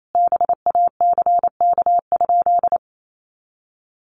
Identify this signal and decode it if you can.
Morse code, heard as a keyed tone, about 28 wpm, 700 Hz. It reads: BACK?
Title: Sentences from Top 200 Words 28wpm